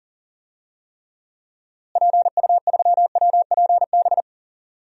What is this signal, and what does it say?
Morse code, heard as a keyed tone, about 40 wpm, 700 Hz. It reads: PU3WPB
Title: DXCC-100 Call Signs - 6 Characters - Repeated 3x - 40wpm